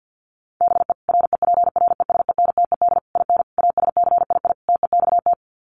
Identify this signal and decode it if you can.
Morse code, heard as a keyed tone, about 48 wpm, 700 Hz. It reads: BE REPRESENTED IN ASCII TEXT
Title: Sets of 5 Words 50wpm